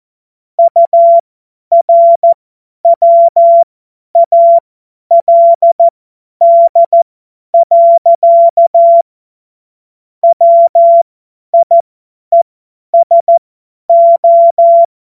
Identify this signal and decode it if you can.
Morse code, heard as a keyed tone, about 14 wpm, spaced as for 10 wpm, 680 Hz. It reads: URWALD. WIESO
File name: Morsekurs / Witze / Witz 07. Februar 2023 50 F70